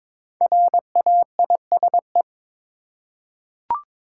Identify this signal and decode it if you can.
Morse code, heard as a keyed tone, about 23 wpm, 700 Hz. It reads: RAISE
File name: 5 Letter Words - First 3 Letters Given - 22wpm